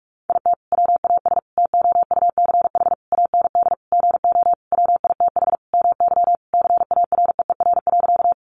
Morse code, heard as a keyed tone, about 44 wpm, 700 Hz. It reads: IT WAS TOUCH AND GO WITH MY CAREER.
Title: Sentences from Top 1200 Words 45wpm